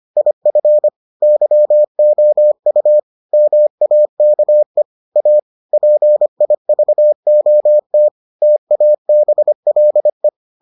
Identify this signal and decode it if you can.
Morse code, heard as a keyed tone, about 25 wpm, 600 Hz. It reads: IF YOU MAKE A PIVOT TABLE